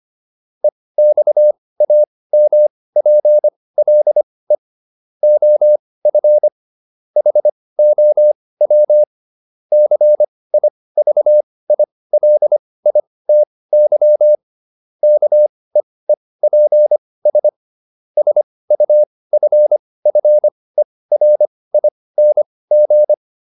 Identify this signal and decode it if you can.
Morse code, heard as a keyed tone, about 25 wpm, spaced as for 18 wpm, 600 Hz. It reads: EXAMPLE OF HOW CIVILITY KEEPS SUFFERING